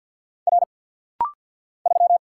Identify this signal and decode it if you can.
Morse code, heard as a keyed tone, about 49 wpm, 700 Hz. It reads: R 3
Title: Single Letter-Number - Rapid-Fire 50wpm